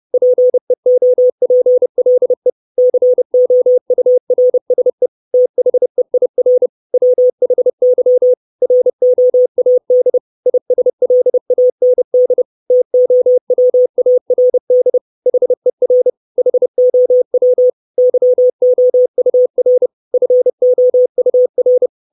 Morse code, heard as a keyed tone, about 30 wpm, 500 Hz. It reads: PEOPLE COURSE THEIR WHY ROAD ISLAND TOWARD HER HOW YOUR FOUR
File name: 632 Words 30WPM